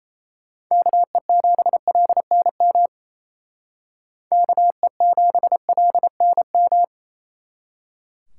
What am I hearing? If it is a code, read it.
Morse code, heard as a keyed tone, about 36 wpm, 700 Hz. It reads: KE7LNM KE7LNM